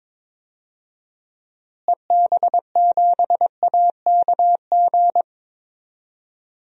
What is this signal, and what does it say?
Morse code, heard as a keyed tone, about 22 wpm, 700 Hz. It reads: EB7AKG
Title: DXCC-100 Call Signs - 6 Characters - Repeated 3x - 22wpm